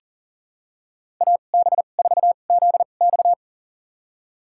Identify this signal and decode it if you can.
Morse code, heard as a keyed tone, about 40 wpm, spaced as for 29 wpm, 700 Hz. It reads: AB4ZX
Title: USA Call Signs - 5 Characters - 2x Character Spacing - 40wpm